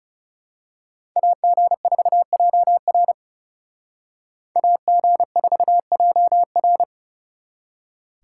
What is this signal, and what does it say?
Morse code, heard as a keyed tone, about 34 wpm, 700 Hz. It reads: AG4JR AG4JR
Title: Speed Racing - USA 2x2 Call Signs 30wpm